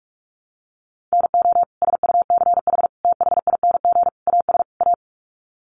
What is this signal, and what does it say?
Morse code, heard as a keyed tone, about 44 wpm, 700 Hz. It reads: NO SUCH THING AS A
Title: Sets of 5 Words Encore 45wpm